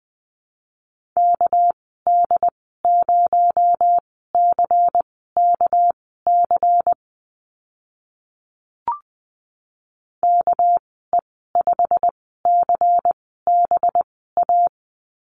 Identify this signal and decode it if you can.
Morse code, heard as a keyed tone, about 20 wpm, spaced as for 14 wpm, 700 Hz. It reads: KD0CKC KE5CBA
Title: 219 Lesson 35g - Number 0 - Call Signs 20wpm